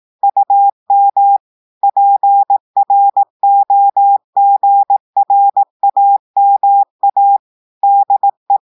Morse code, heard as a keyed tone, about 18 wpm, 800 Hz. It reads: UM PROGRAMA DE
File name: portugais texte en clair 2 vitesse 18 mots minutes